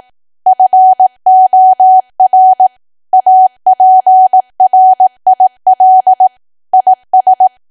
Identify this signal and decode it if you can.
Morse code, heard as a keyed tone, about 18 wpm, 750 Hz. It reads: FOR APRIL IS
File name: cwa / W1AW-Level2 / 18WPM